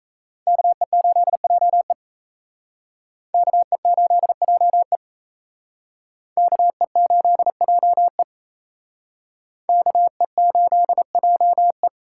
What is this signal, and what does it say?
Morse code, heard as a keyed tone, about 35 wpm, 700 Hz. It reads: KE8JE KE8JE KE8JE KE8JE
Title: Speed Racing - USA 2x2 Call Signs 28wpm